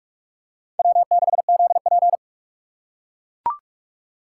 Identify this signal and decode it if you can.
Morse code, heard as a keyed tone, about 42 wpm, 700 Hz. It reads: W6ZP